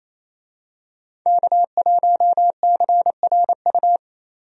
Morse code, heard as a keyed tone, about 27 wpm, 700 Hz. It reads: K1CRU
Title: USA Call Signs - 5 Characters - Repeated 3x - 28wpm